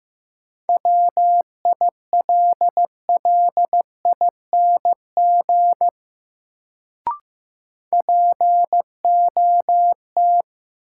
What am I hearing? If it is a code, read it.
Morse code, heard as a keyed tone, about 16 wpm, 700 Hz. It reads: WILLING POT
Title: Sentences from Top 1000 Words - New Word Drill - 15wpm